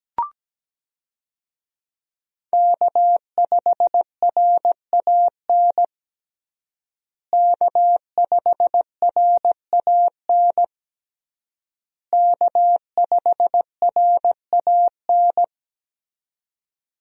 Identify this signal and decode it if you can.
Morse code, heard as a keyed tone, about 17 wpm, 700 Hz. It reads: K5RAN K5RAN K5RAN
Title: USA Call Signs - 5 Characters - Repeated 3x - 17wpm